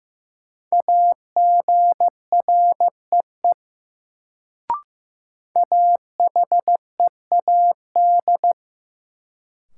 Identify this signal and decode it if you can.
Morse code, heard as a keyed tone, about 15 wpm, 700 Hz. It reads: AGREE AHEAD
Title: Sentences from Top 1000 Words - Review Part 6 - New Word Drill - 15wpm